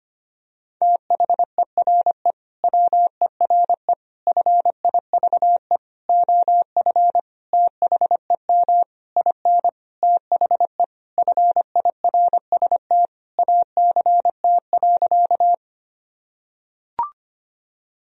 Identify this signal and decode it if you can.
Morse code, heard as a keyed tone, about 25 wpm, 700 Hz. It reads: THERE WERE FIVE OF THEM IN THE FIRST ACT.
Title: Sentences from Top 1000 Words - Review Part 2 - 25wpm